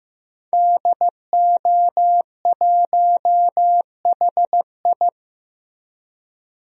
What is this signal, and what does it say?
Morse code, heard as a keyed tone, about 15 wpm, 700 Hz. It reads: DO1HI
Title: DXCC-100 Call Signs - 5 Characters - Repeated 2x - 15wpm